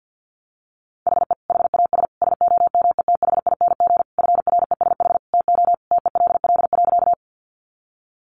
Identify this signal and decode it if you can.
Morse code, heard as a keyed tone, about 50 wpm, 700 Hz. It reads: HE HAS SOMETHING FRESH TO TELL.
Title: Sentences from Top 1200 Words 50wpm